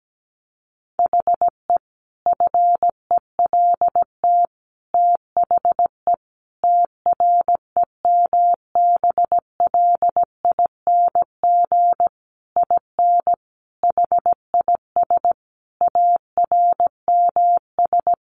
Morse code, heard as a keyed tone, about 17 wpm, 700 Hz. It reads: HE FELT THE TREMBLING IN HIS ARMS
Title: A Sound Of Thunder 17wpm